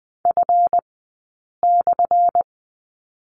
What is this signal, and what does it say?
Morse code, heard as a keyed tone, about 20 wpm, spaced as for 8 wpm, 690 Hz. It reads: F/
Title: 217 Lesson 35d - Number 0 - Two Characters 20wpm